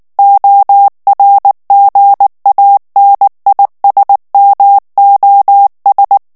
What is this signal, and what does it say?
Morse code, heard as a keyed tone, about 19 wpm, 800 Hz. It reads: ORGANISMOS
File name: morse mp3 / ESPAGNOL TEXTE EN CLAIR 2 VITESSE 18 MOTS MINUTES